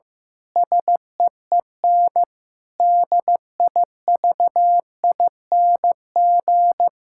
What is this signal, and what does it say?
Morse code, heard as a keyed tone, about 15 wpm, 700 Hz. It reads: SEEN DIVING
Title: Pangram Challenge v2 - 15wpm